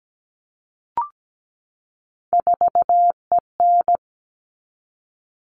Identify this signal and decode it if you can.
Morse code, heard as a keyed tone, about 17 wpm, 700 Hz. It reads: 4EN
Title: All Cut Numbers - 3-Digit 17wpm